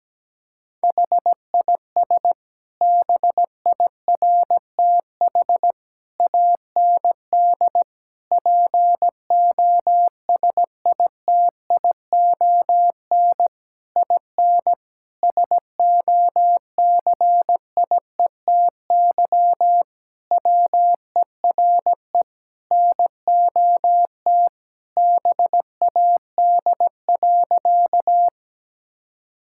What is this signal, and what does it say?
Morse code, heard as a keyed tone, about 17 wpm, 700 Hz. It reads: HIS BIRTH AND POSITION IN SOCIETY WERE NOT BAD.
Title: Sentences from Top 1000 Words - Review Part 5 - 17wpm